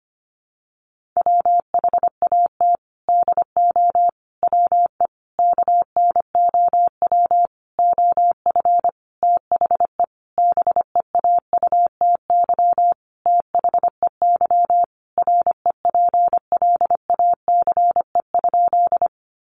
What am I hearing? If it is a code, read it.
Morse code, heard as a keyed tone, about 25 wpm, 700 Hz. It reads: WHAT DO WE KNOW OF THE BEAUTY THEY REPLACE?
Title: Sentences from Top 1400 Words 25wpm